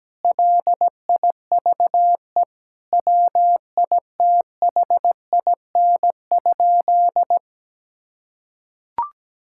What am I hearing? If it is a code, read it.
Morse code, heard as a keyed tone, about 17 wpm, 700 Hz. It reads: LIVE WITHIN?
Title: Sentences from Top 1000 Words - Review Part 6 - 17wpm